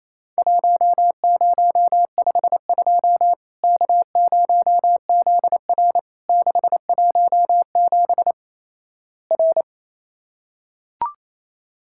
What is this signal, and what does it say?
Morse code, heard as a keyed tone, about 28 wpm, 700 Hz. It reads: 1052 K0ZR 617 R
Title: Worked All Europe DX Contest - QTC Exchange - 28wpm